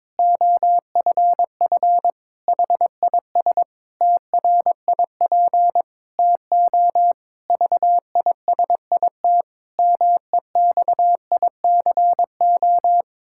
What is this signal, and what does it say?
Morse code, heard as a keyed tone, about 22 wpm, 700 Hz. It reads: OFF HIS TRIP TO VISIT MEXICO
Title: Pangram Challenge v2 - 22wpm